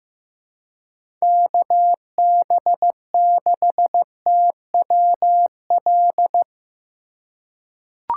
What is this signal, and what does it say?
Morse code, heard as a keyed tone, about 15 wpm, 700 Hz. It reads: KB6TWL
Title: USA Call Signs - 2x Character Spacing - 15wpm